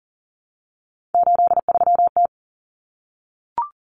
Morse code, heard as a keyed tone, about 40 wpm, 700 Hz. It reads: 83T
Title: Common Cut Numbers - 3-Digit 40wpm